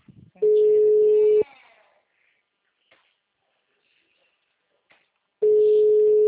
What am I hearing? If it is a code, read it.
Ringback tone (Europe, 425 Hz), with speech.